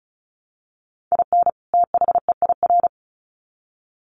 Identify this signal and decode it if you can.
Morse code, heard as a keyed tone, about 34 wpm, 700 Hz. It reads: IN THEIR
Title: Sets of 2 Words 35wpm